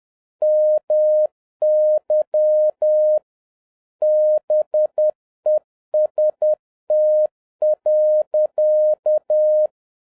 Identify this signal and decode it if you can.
Morse code, heard as a keyed tone, about 10 wpm, 610 Hz. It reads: MY BEST.